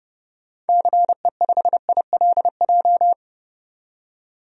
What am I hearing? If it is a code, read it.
Morse code, heard as a keyed tone, about 30 wpm, 700 Hz. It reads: CE5ILJ
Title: DXCC-100 Call Signs - 6 Characters - 30wpm